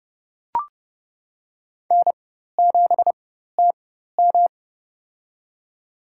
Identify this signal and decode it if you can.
Morse code, heard as a keyed tone, about 31 wpm, spaced as for 14 wpm, 700 Hz. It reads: N7TM
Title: USA Call Signs - 4 Characters - 4x Character Spacing - 30wpm